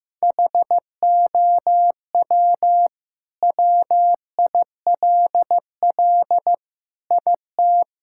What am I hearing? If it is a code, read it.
Morse code, heard as a keyed tone, about 16 wpm, 700 Hz. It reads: HOW WILL IT
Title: Sentences from Top 1000 Words - Review Part 6 - 15wpm